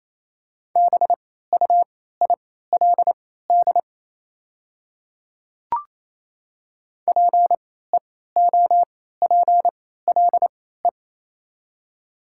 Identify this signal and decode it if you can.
Morse code, heard as a keyed tone, about 27 wpm, spaced as for 16 wpm, 690 Hz. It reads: BUILD PEOPLE
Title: 3x Character Spacing - Top 300 Words - 28wpm